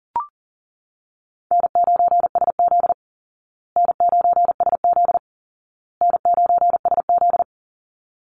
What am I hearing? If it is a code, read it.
Morse code, heard as a keyed tone, about 40 wpm, 700 Hz. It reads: N9SZ N9SZ N9SZ
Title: Easy Call Signs Repeated 3x - 40wpm